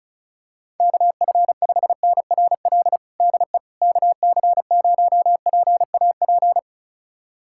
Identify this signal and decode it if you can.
Morse code, heard as a keyed tone, about 34 wpm, 700 Hz. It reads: KF5NRL DE KC0PAP
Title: USA Callsign DE Callsign 35wpm